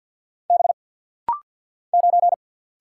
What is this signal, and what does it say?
Morse code, heard as a keyed tone, about 48 wpm, 700 Hz. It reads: B 9
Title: Single Letter-Number - Rapid-Fire 50wpm